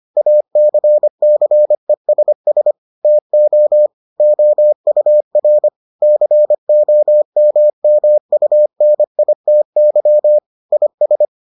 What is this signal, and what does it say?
Morse code, heard as a keyed tone, about 25 wpm, 600 Hz. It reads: ACCESS TO OUR COMMUNITY IS